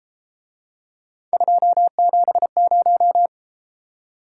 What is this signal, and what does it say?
Morse code, heard as a keyed tone, about 31 wpm, 700 Hz. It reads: 270